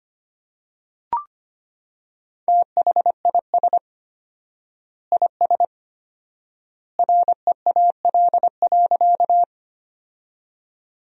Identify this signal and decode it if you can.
Morse code, heard as a keyed tone, about 25 wpm, 700 Hz. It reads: THIS IS REAL.
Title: 4x Word-Spacing - Sentences from Top 300 Words - 25wpm